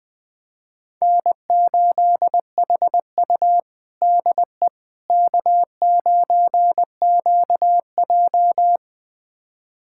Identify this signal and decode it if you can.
Morse code, heard as a keyed tone, about 20 wpm, 700 Hz. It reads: N8HU DE K9QJ
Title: USA Callsign DE Callsign - 4 Characters - 20wpm